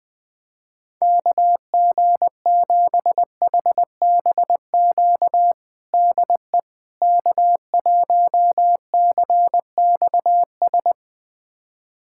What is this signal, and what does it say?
Morse code, heard as a keyed tone, about 21 wpm, 700 Hz. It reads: KG7HBQ DE K1CXS
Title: USA Callsign DE Callsign 20wpm